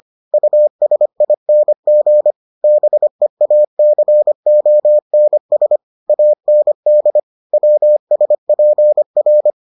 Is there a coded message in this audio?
Yes: Morse code from 0.3 to 9.5 s, heard as a keyed tone, about 25 wpm, 600 Hz. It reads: USING BEACONS AND WSPR